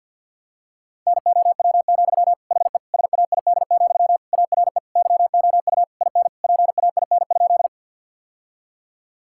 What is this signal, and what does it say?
Morse code, heard as a keyed tone, about 49 wpm, 700 Hz. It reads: NOW, HE SAID, ARE YOU IN PAIN?
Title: Sentences from Top 1600 Words 50wpm